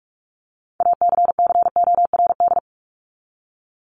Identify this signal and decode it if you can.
Morse code, heard as a keyed tone, about 44 wpm, 700 Hz. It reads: ACCORD
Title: ING Suffix - Top 100 Words with Base Words 45wpm